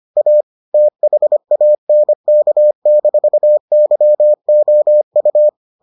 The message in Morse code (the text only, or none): A THANK-YOU